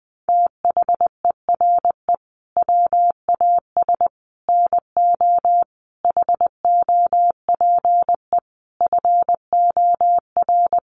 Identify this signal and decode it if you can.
Morse code, heard as a keyed tone, about 20 wpm, 700 Hz. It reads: THERE WAS NO HOPE FOR
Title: Sentences from Top 1000 Words 20wpm